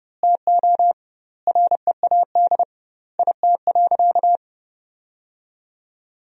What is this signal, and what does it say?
Morse code, heard as a keyed tone, about 30 wpm, 700 Hz. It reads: TO READ IT.
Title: 2x Word-Spacing - Sentences from Top 500 Words - 30wpm